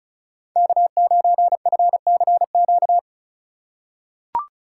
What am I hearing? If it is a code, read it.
Morse code, heard as a keyed tone, about 36 wpm, 700 Hz. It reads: K9FCQ
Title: USA Call Signs - 5 Characters - 3x Character Spacing - 35wpm